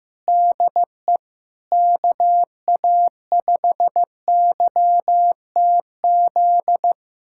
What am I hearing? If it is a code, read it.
Morse code, heard as a keyed tone, about 16 wpm, 700 Hz. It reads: DE KA5YTZ
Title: USA Callsign DE Callsign - 6 Characters - 15wpm